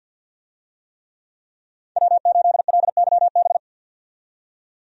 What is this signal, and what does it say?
Morse code, heard as a keyed tone, about 50 wpm, 700 Hz. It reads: W8LYB
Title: USA Call Signs - 5 Characters - 50wpm